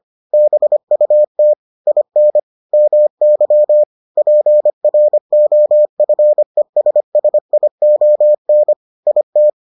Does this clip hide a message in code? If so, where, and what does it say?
Yes: Morse code from 0.3 to 9.5 s, heard as a keyed tone, about 25 wpm, 600 Hz. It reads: BUT IN MY PROFESSION IT